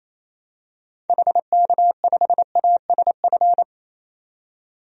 Morse code, heard as a keyed tone, about 28 wpm, 700 Hz. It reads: HK5ASF
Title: DXCC-100 Call Signs - 6 Characters - 28wpm